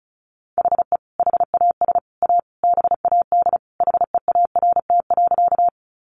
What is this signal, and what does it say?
Morse code, heard as a keyed tone, about 34 wpm, 700 Hz. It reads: HE HAS A BAD HEART.
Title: Sentences from Top 1200 Words 35wpm